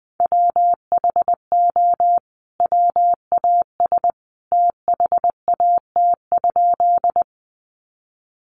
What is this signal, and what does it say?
Morse code, heard as a keyed tone, about 20 wpm, 700 Hz. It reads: WHO WAS THAT?
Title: Sentences from Top 100 Words 20wpm